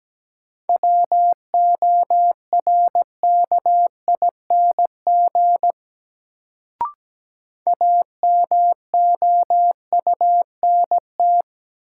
Morse code, heard as a keyed tone, about 17 wpm, 700 Hz. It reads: WORKING AMOUNT
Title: Sentences from Top 700 Words - New Word Drill - 17wpm